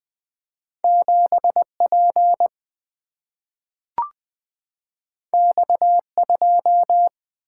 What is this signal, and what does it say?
Morse code, heard as a keyed tone, about 20 wpm, 700 Hz. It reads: 7 P X 2
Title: Single-Letter Single-Number 20wpm